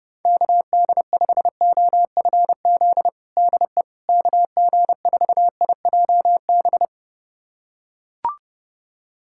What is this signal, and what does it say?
Morse code, heard as a keyed tone, about 31 wpm, 700 Hz. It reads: KD5OFZ DE KG4IJB
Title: USA Callsign DE Callsign - 6 Characters - 30wpm